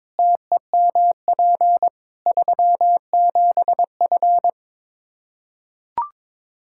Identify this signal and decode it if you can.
Morse code, heard as a keyed tone, about 22 wpm, 700 Hz. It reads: TEMP 37F
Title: K1USN SST QSO Practice 22wpm